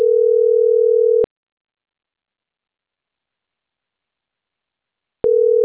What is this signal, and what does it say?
Ringback tone (North America), 440 + 480 Hz.
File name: CENTRAL CAEA / out agcaea03